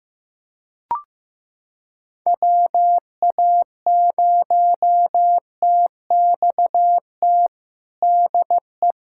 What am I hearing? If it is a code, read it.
Morse code, heard as a keyed tone, about 15 wpm, 700 Hz. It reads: WA0TXT DE
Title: USA Callsign DE Callsign 15wpm